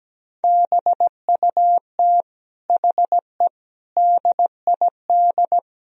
Morse code, heard as a keyed tone, about 17 wpm, 700 Hz. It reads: BUT HE DID
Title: Speed-Racing - Sets of 4 Words 17wpm